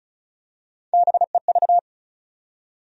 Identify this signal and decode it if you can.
Morse code, heard as a keyed tone, about 34 wpm, 700 Hz. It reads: BEV
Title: USA Common Names - Max 5 Letters - 35wpm